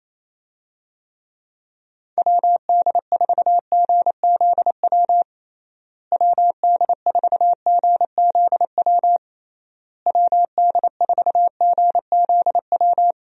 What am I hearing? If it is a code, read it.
Morse code, heard as a keyed tone, about 28 wpm, 700 Hz. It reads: WD4GZW WD4GZW WD4GZW
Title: USA Call Signs - 6 Characters - Repeated 3x - 28wpm